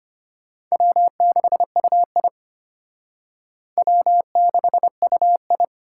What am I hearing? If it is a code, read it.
Morse code, heard as a keyed tone, about 33 wpm, 700 Hz. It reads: W6UI W6UI